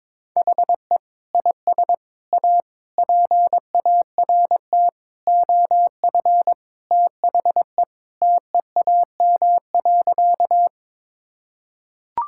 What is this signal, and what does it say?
Morse code, heard as a keyed tone, about 22 wpm, 700 Hz. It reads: HE IS A PART OF THE TEAM.
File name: practice / 22wpm / Sentences with 4-Word Phrases - 22wpm